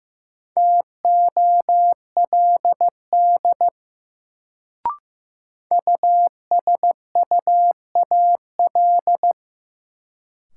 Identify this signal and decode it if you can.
Morse code, heard as a keyed tone, about 15 wpm, 700 Hz. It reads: TOLD USUAL
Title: Sentences from Top 400 Words - New Word Drill - 15wpm